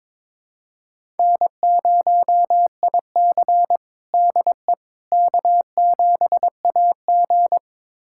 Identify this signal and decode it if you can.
Morse code, heard as a keyed tone, about 22 wpm, 700 Hz. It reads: N0IC DE K7AG
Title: USA Callsign DE Callsign - 4 Characters - 22wpm